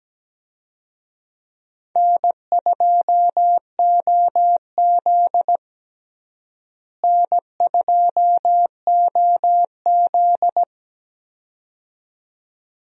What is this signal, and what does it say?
Morse code, heard as a keyed tone, about 17 wpm, 700 Hz. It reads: N2OZ N2OZ
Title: USA Call Signs - 4 Characters - Repeated 2x - 17wpm